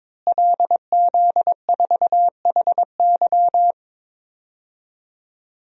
Morse code, heard as a keyed tone, about 22 wpm, 700 Hz. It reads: LZ4HY
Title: DXCC-100 Call Signs - 5 Characters - Repeated 3x - 22wpm